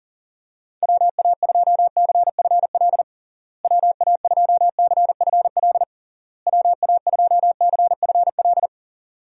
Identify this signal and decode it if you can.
Morse code, heard as a keyed tone, about 37 wpm, 700 Hz. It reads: WA2CFL WA2CFL WA2CFL
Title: USA Call Signs - 6 Characters - Repeated 3x - 40wpm